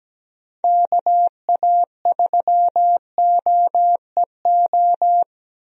Morse code, heard as a keyed tone, about 17 wpm, 700 Hz. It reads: KA3OEO
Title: USA Call Signs - 6 Characters - Repeated 2x - 17wpm